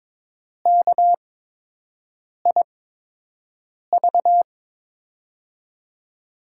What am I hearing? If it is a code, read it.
Morse code, heard as a keyed tone, about 22 wpm, spaced as for 6 wpm, 700 Hz. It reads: KIV
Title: USA Call Signs - 8x Character Spacing - 22wpm